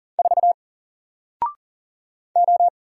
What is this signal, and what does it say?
Morse code, heard as a keyed tone, about 40 wpm, 700 Hz. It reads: 4 O
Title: Single Letter-Number Alt 40wpm